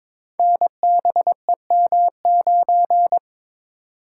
Morse code, heard as a keyed tone, about 22 wpm, 700 Hz. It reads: NBEM9
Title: Code Groups - 5 Letter:Numbers - 22wpm